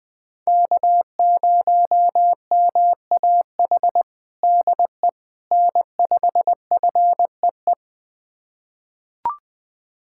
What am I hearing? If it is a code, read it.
Morse code, heard as a keyed tone, about 20 wpm, 700 Hz. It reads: K0MAH DE N5FEE
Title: USA Callsign DE Callsign 20wpm